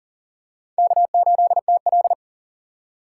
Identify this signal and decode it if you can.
Morse code, heard as a keyed tone, about 40 wpm, 700 Hz. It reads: K8TL
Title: USA Call Signs - 3x Character Spacing - 40wpm